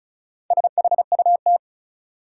Morse code, heard as a keyed tone, about 36 wpm, 700 Hz. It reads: SHUT
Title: Four Letter Words - 35wpm